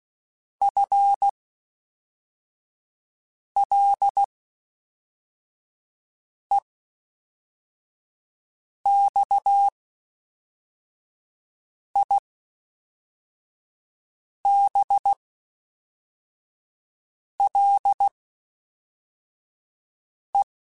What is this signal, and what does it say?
Morse code, heard as a keyed tone, about 16 wpm, spaced as for 4 wpm, 800 Hz. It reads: FLEXIBLE
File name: Spaced / FL Runs sp mp3s / FL Runs sp @5